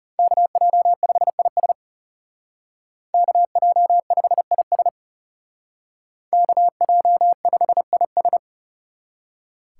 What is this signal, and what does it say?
Morse code, heard as a keyed tone, about 35 wpm, 700 Hz. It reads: KJ5IS KJ5IS KJ5IS
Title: Speed Racing - USA 2x2 Call Signs 30wpm